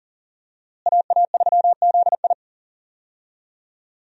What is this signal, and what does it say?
Morse code, heard as a keyed tone, about 40 wpm, 700 Hz. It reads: AA3ZI
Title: USA Call Signs - 5 Characters - Repeated 3x - 40wpm